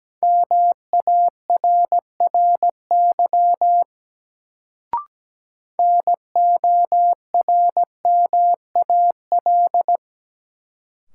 Morse code, heard as a keyed tone, about 17 wpm, 700 Hz. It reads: MARRY NORMAL
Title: Sentences from Top 1200 Words - New Word Drill - 17wpm